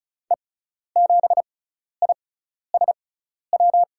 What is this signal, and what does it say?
Morse code, heard as a keyed tone, about 36 wpm, spaced as for 12 wpm, 690 Hz. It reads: E7ISW
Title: USA Call Signs - 6 Characters - 6x Character Spacing - 35wpm